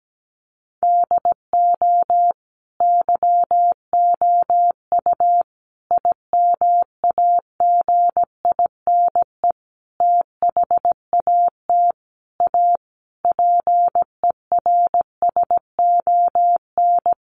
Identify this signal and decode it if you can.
Morse code, heard as a keyed tone, about 17 wpm, 700 Hz. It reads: DO YOU IMAGINE THAT A PERSON
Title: Sentences from Top 1400 Words 17wpm